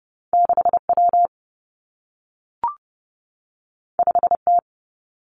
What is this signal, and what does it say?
Morse code, heard as a keyed tone, about 30 wpm, 700 Hz. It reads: 6 W 5 T
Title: Two Challenging Letter-Number Combinations 30wpm